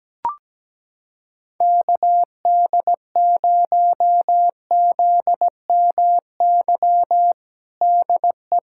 Morse code, heard as a keyed tone, about 17 wpm, 700 Hz. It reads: KD0ZMY DE
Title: USA Callsign DE Callsign - 6 Characters - 17wpm